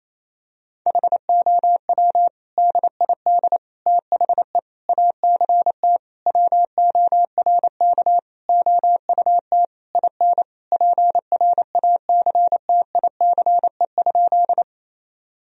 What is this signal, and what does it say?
Morse code, heard as a keyed tone, about 28 wpm, 700 Hz. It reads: HOW DID THE ACT WORK OUT IN PRACTICE?
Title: Sentences from Top 1000 Words - Review Part 4 - 28wpm